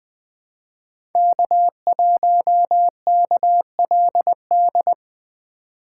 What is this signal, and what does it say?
Morse code, heard as a keyed tone, about 22 wpm, 700 Hz. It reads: K1KLD
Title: Speed Racing - USA 1x3 Call Signs 20wpm